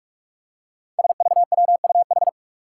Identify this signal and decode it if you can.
Morse code, heard as a keyed tone, about 44 wpm, 700 Hz. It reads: SVWUH
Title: Five Challenging Letter-Number Combinations 45wpm